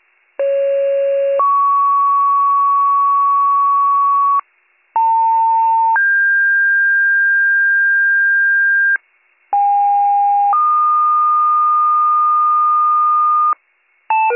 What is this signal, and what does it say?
Two-tone page, heard as a one-second tone, then a three-second tone, about 550 then 1120 Hz, about 900 then 1600 Hz, about 810 then 1220 Hz. It is heard from 0.4 to 13.5 s.